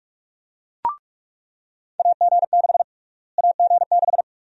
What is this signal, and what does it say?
Morse code, heard as a keyed tone, about 45 wpm, 700 Hz. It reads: AG6 AG6
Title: USA Call Sign Prefixes - Repeated 2x - 45wpm